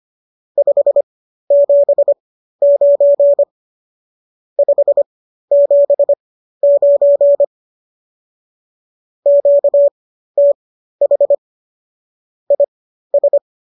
Morse code, heard as a keyed tone, about 25 wpm, spaced as for 13 wpm, 570 Hz. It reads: 579 579 QTH IS